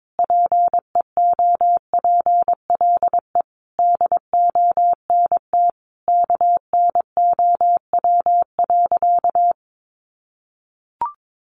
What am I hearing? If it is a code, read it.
Morse code, heard as a keyed tone, about 22 wpm, 700 Hz. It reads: PEOPLE DONT KNOW.
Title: Sentences from Top 600 Words 22wpm